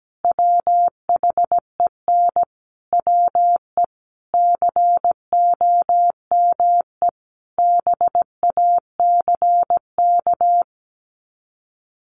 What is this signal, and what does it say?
Morse code, heard as a keyed tone, about 17 wpm, 700 Hz. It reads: WHEN WE COME BACK
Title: Sets of 5 Words 17wpm